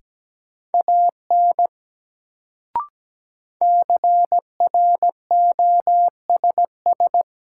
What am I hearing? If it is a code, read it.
Morse code, heard as a keyed tone, about 17 wpm, 700 Hz. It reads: AN CROSS
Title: Sentences from Top 300 Words - New Word Drill - 17wpm